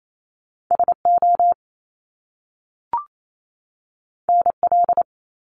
Morse code, heard as a keyed tone, about 28 wpm, 700 Hz. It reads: SO NL
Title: Top Two Letter Combinations in Words 28wpm